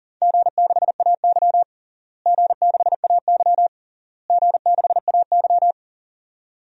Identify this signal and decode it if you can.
Morse code, heard as a keyed tone, about 40 wpm, 700 Hz. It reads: G6AY G6AY G6AY
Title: DXCC-100 Call Signs - 4 Characters - Repeated 3x - 40wpm